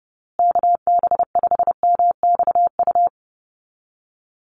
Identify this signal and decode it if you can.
Morse code, heard as a keyed tone, about 29 wpm, 700 Hz. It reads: KB5MXU
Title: USA Call Signs Repeated 2x 30wpm